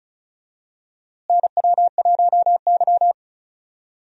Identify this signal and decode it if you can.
Morse code, heard as a keyed tone, about 36 wpm, 700 Hz. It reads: NW1Y
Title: USA Call Signs - 4 Characters - Repeated 2x - 35wpm